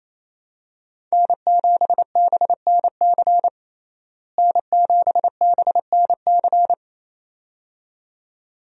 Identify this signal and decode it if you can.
Morse code, heard as a keyed tone, about 28 wpm, 700 Hz. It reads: N7BNC N7BNC
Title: USA Call Signs - 5 Characters - Repeated 2x - 28wpm